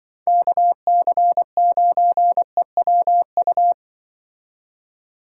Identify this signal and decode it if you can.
Morse code, heard as a keyed tone, about 25 wpm, 700 Hz. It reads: KC9EWU
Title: Speed Racing - USA Call Signs 20wpm